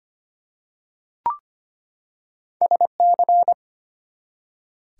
Morse code, heard as a keyed tone, about 25 wpm, 700 Hz. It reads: SC
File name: practice / 25wpm / ARRL Sections v2 - 25wpm